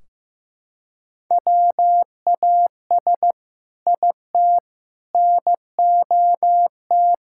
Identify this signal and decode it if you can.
Morse code, heard as a keyed tone, about 15 wpm, 700 Hz. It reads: WAS IT NOT
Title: Sentences from Top 1600 Words 15wpm